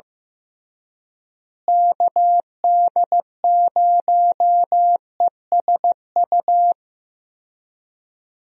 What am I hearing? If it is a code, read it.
Morse code, heard as a keyed tone, about 15 wpm, 700 Hz. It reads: KD0ESU
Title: USA Call Signs - 6 Characters - Repeated 3x - 15wpm